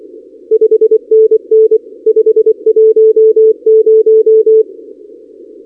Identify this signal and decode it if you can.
Morse code, heard as a keyed tone, about 24 wpm, 430 Hz. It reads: 5NN 510
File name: contests / 2019-cw-wpx-kt5j-k5pi- -2330 0-32